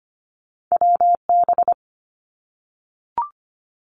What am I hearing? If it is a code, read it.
Morse code, heard as a keyed tone, about 25 wpm, 700 Hz. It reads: W B
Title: Two Challenging Letter-Number Combinations 25wpm